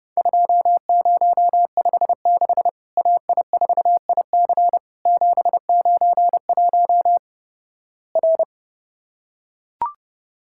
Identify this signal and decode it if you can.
Morse code, heard as a keyed tone, about 30 wpm, 700 Hz. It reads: 2056 AI4IC 791 R
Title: Worked All Europe DX Contest - QTC Exchange - 30wpm